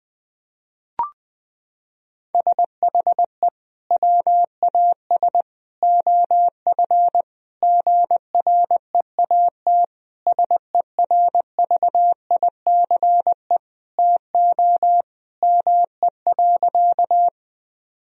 Morse code, heard as a keyed tone, about 21 wpm, 700 Hz. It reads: SHE WAS OF GREAT SERVICE TO ME.
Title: Sentences from Top 1000 Words - Review Part 4 - 20wpm